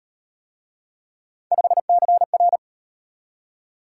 Morse code, heard as a keyed tone, about 36 wpm, 700 Hz. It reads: 5CR